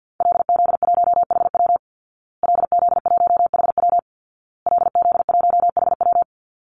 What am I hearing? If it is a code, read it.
Morse code, heard as a keyed tone, about 50 wpm, 700 Hz. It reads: LZ1HW LZ1HW LZ1HW
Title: Call Signs Repeated 2x - 50wpm